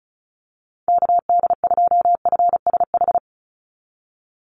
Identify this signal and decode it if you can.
Morse code, heard as a keyed tone, about 34 wpm, 700 Hz. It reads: KD2FSH
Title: Call Signs Repeated 2x - 35wpm